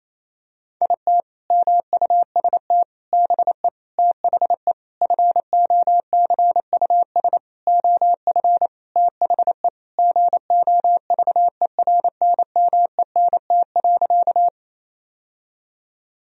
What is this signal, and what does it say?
Morse code, heard as a keyed tone, about 28 wpm, 700 Hz. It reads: IT MUST BE THE FOCUS OF THE GOVERNMENT.
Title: Sentences from Top 1000 Words - Review Part 4 - 28wpm